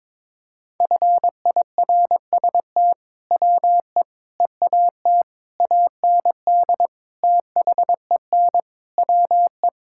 Morse code, heard as a keyed tone, about 22 wpm, 700 Hz. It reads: FIRST WE EAT AND THEN WE
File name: practice / 22wpm / Sentences with 2-Word Phrases - 22wpm